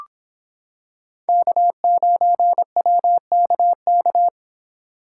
Morse code, heard as a keyed tone, about 25 wpm, 700 Hz. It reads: K9WKK